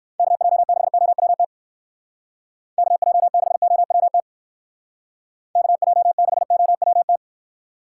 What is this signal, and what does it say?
Morse code, heard as a keyed tone, about 59 wpm, 700 Hz. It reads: KJ6OWT KJ6OWT KJ6OWT